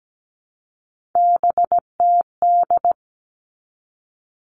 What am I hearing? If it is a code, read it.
Morse code, heard as a keyed tone, about 17 wpm, 700 Hz. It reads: BTD
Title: All Cut Numbers - 3-Digit 17wpm